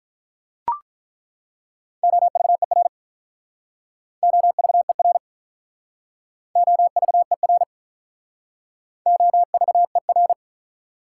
Morse code, heard as a keyed tone, about 48 wpm, 700 Hz. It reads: OVER OVER OVER OVER